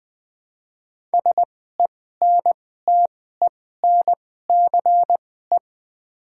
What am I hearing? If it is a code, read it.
Morse code, heard as a keyed tone, about 20 wpm, spaced as for 14 wpm, 700 Hz. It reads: SENTENCE
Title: 2x Character Spacing - Top 200 Words - 20wpm